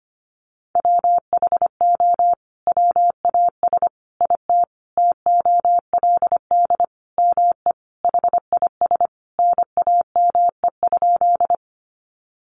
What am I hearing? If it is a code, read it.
Morse code, heard as a keyed tone, about 25 wpm, 700 Hz. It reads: WHO WAS IT TOLD ME HIS NAME?
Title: Sentences from Top 400 Words 25wpm